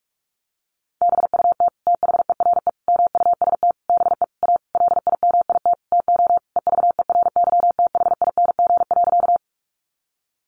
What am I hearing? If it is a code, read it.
Morse code, heard as a keyed tone, about 44 wpm, 700 Hz. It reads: BUT THERE MUST BE A LIMIT TO EVERYTHING.
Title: Sentences from Top 1500 Words 45wpm